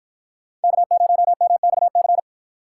Morse code, heard as a keyed tone, about 48 wpm, 700 Hz. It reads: K0MXC